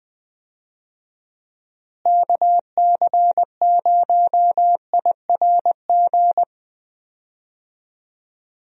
Morse code, heard as a keyed tone, about 20 wpm, 700 Hz. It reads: KC0IRG
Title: USA Call Signs - 6 Characters - Repeated 2x - 20wpm